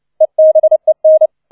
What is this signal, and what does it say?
Morse code, heard as a keyed tone, about 29 wpm, 630 Hz. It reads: EBEN